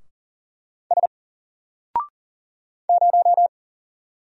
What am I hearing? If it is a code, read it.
Morse code, heard as a keyed tone, about 40 wpm, 700 Hz. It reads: S 0
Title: Single Letter-Number 40wpm